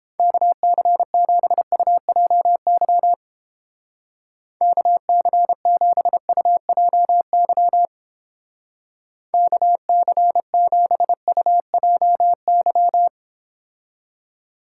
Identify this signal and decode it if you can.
Morse code, heard as a keyed tone, about 29 wpm, 700 Hz. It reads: KC7UJY KC7UJY KC7UJY